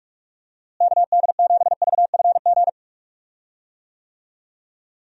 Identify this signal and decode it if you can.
Morse code, heard as a keyed tone, about 44 wpm, 700 Hz. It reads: KD7VFG
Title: USA Call Signs - 6 Characters - Repeated 2x - 45wpm